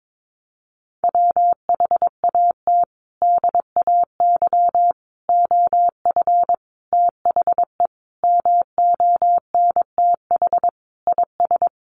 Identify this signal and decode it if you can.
Morse code, heard as a keyed tone, about 22 wpm, 700 Hz. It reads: WHAT DAY OF THE MONTH IS
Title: Sentences from Top 800 Words 22wpm